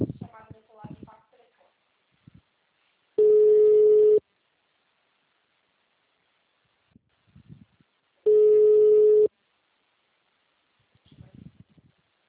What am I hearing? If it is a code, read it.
Ringback tone (Europe, 425 Hz), with speech.
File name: out- -2002- - - .295